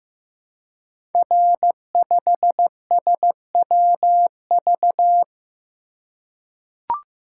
Morse code, heard as a keyed tone, about 15 wpm, 700 Hz. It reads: R5SWV
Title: Code Groups - 5 Letter:Numbers - 15wpm